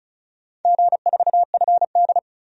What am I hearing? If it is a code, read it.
Morse code, heard as a keyed tone, about 36 wpm, 700 Hz. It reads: G4FD
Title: DXCC-100 Call Signs - 4 Characters - 35wpm